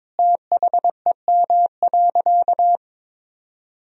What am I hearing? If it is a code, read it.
Morse code, heard as a keyed tone, about 21 wpm, 700 Hz. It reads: THEM.
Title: Sentences from Top 200 Words and Abbreviations 22wpm